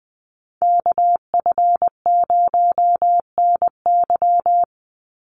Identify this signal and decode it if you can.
Morse code, heard as a keyed tone, about 20 wpm, 700 Hz. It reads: KF0NY
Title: USA Call Signs 20wpm